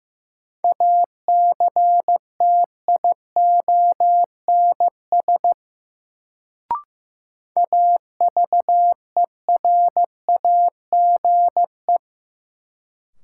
Morse code, heard as a keyed tone, about 15 wpm, 700 Hz. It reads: ACTIONS AVERAGE
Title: Sentences from Top 1000 Words - Review Part 6 - New Word Drill - 15wpm